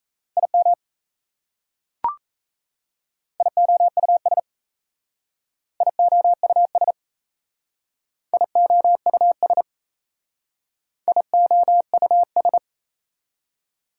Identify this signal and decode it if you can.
Morse code, heard as a keyed tone, about 34 wpm, 700 Hz. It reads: IM IOUS IOUS IOUS IOUS
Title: Speed-Racing Prefixes and Suffixes 28wpm